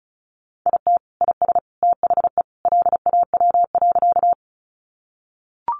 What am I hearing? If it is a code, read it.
Morse code, heard as a keyed tone, about 34 wpm, 700 Hz. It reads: IT IS THE LAW.
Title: Sentences from Top 600 Words 35wpm